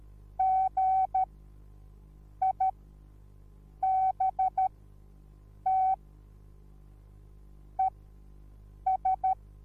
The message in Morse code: GIBT ES